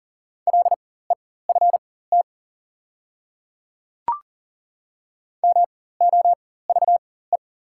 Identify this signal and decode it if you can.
Morse code, heard as a keyed tone, about 41 wpm, spaced as for 19 wpm, 690 Hz. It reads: LEFT MOVE
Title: 4x Character Spacing - Top 500 Words - 40wpm